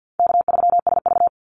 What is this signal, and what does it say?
Morse code, heard as a keyed tone, about 50 wpm, 700 Hz. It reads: K3SV
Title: Easy Call Signs Repeated 2x - 50wpm